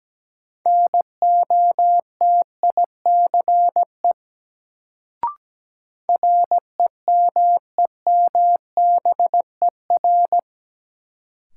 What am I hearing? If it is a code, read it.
Morse code, heard as a keyed tone, about 17 wpm, 700 Hz. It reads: NOTICE REMEMBER
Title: Sentences from Top 400 Words - New Word Drill - 17wpm